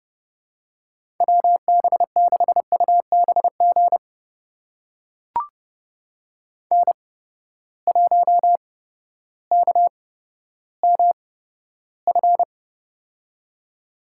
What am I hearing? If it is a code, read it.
Morse code, heard as a keyed tone, about 30 wpm, spaced as for 14 wpm, 700 Hz. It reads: WB6UBG N1KMF
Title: USA Call Signs - 8x Character Spacing - 30wpm